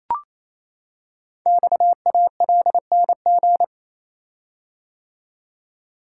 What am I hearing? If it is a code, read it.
Morse code, heard as a keyed tone, about 29 wpm, 700 Hz. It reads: XALNG
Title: Code Groups - 5 Letters - 28wpm